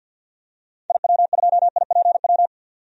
Morse code, heard as a keyed tone, about 51 wpm, 700 Hz. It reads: IW2IPW
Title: DXCC-100 Call Signs - 6 Characters - 50wpm